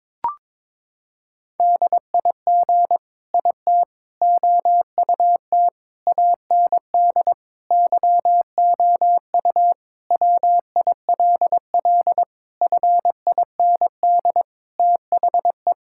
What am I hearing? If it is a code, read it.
Morse code, heard as a keyed tone, about 22 wpm, 700 Hz. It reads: DIG IT OUT AND YOU WILL FIND THE
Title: Sentences from Top 1600 Words 22wpm